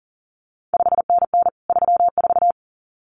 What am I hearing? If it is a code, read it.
Morse code, heard as a keyed tone, about 38 wpm, 700 Hz. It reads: 5NN 34
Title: CQ WW DX Contest Exchange Encore - 40wpm